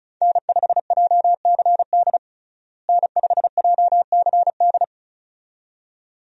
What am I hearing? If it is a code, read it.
Morse code, heard as a keyed tone, about 34 wpm, 700 Hz. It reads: N5JCD N5JCD
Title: USA Call Signs - 5 Characters - Repeated 2x - 35wpm